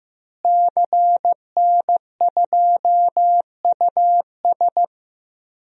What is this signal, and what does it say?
Morse code, heard as a keyed tone, about 15 wpm, 700 Hz. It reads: CN2US
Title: DXCC-100 Call Signs - 5 Characters - Repeated 2x - 15wpm